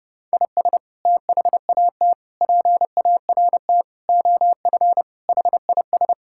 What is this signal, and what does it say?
Morse code, heard as a keyed tone, about 30 wpm, 700 Hz. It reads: IS THAT PART OF HIS
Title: Sentences from Top 1000 Words - Review Part 5 - 30wpm